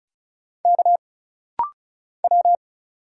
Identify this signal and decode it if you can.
Morse code, heard as a keyed tone, about 35 wpm, 700 Hz. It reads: K W
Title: Single Letter-Number - Rapid-Fire 35wpm